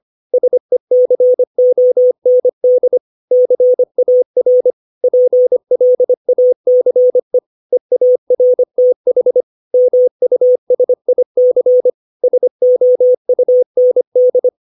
Morse code, heard as a keyed tone, about 25 wpm, 500 Hz. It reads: SECOND CAR PLACE EARTH MUSIC SOUND
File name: 632 words / 632 Words 25WPM